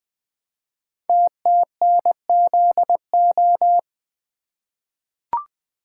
Morse code, heard as a keyed tone, about 20 wpm, 700 Hz. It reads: TTNZO
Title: Code Groups - 5 Letters - 20wpm